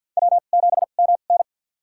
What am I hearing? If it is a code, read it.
Morse code, heard as a keyed tone, about 52 wpm, spaced as for 37 wpm, 700 Hz. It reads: W7MN
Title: USA Call Signs - 4 Characters - 2x Character Spacing - 50wpm